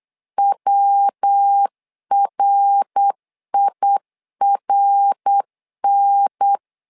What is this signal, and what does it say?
Morse code, heard as a keyed tone, about 8 wpm, 800 Hz. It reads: WRIRN
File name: LEZIONI / 3 MWRO / 3 MWR 8wpm 2444